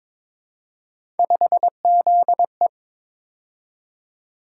Morse code, heard as a keyed tone, about 23 wpm, 700 Hz. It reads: 5ZE
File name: practice / 22wpm / Three Letter-Number Combinations - 22wpm